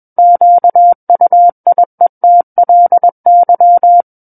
Morse code, heard as a keyed tone, about 21 wpm, 700 Hz. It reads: QUIETLY
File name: Book One / 12-Chapter 12 - 21 WPM